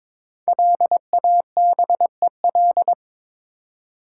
Morse code, heard as a keyed tone, about 22 wpm, 700 Hz. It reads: LABEL
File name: 5 Letter Words - First 3 Letters Given - 22wpm